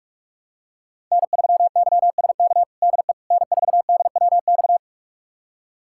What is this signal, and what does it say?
Morse code, heard as a keyed tone, about 45 wpm, 700 Hz. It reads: N3YSK DE N4DWX
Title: USA Callsign DE Callsign - 5 Characters - 45wpm